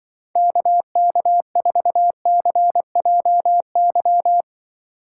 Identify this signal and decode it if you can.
Morse code, heard as a keyed tone, about 24 wpm, 700 Hz. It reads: KK4CJY
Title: Speed Racing - USA Call Signs 20wpm